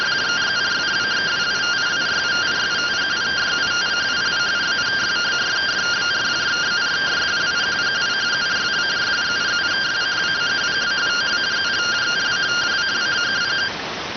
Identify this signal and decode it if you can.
Two-tone teleprinter signal, about 170 Hz shift.